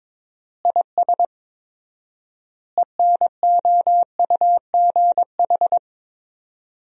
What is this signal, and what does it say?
Morse code, heard as a keyed tone, about 22 wpm, 700 Hz. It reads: IS ENOUGH
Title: 4x Word-Spacing - Sentences from Top 400 Words - 22wpm